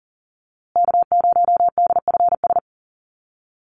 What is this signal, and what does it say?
Morse code, heard as a keyed tone, about 40 wpm, 700 Hz. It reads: K0DFS
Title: USA Call Signs Repeated 3x 40wpm